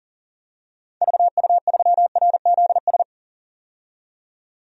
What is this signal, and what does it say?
Morse code, heard as a keyed tone, about 40 wpm, 700 Hz. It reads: VU3RZS
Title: DXCC-100 Call Signs - 6 Characters - 40wpm